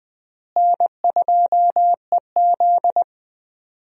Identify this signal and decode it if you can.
Morse code, heard as a keyed tone, about 21 wpm, 700 Hz. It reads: N2EZ
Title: USA Call Signs - 4 Characters - Repeated 3x - 20wpm